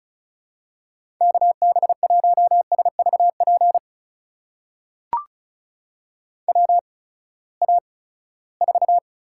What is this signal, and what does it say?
Morse code, heard as a keyed tone, about 36 wpm, spaced as for 20 wpm, 700 Hz. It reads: KB1SVP WA4
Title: USA Call Signs - 6 Characters - 8x Character Spacing - 35wpm